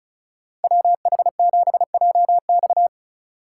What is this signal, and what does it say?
Morse code, heard as a keyed tone, about 44 wpm, 700 Hz. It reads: WH7JX